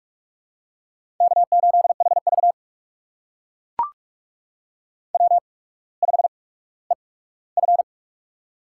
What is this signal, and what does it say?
Morse code, heard as a keyed tone, about 44 wpm, spaced as for 19 wpm, 700 Hz. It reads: K8HV W5EF
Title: USA Call Signs - 4 Characters - 8x Character Spacing - 45wpm